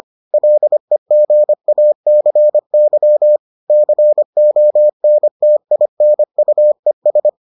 Morse code, heard as a keyed tone, about 25 wpm, 600 Hz. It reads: LEGACY CONTINUES